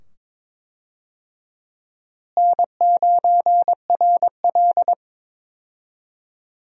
Morse code, heard as a keyed tone, about 22 wpm, 700 Hz. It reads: N9RL
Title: USA Call Signs - 4 Characters - Repeated 2x - 22wpm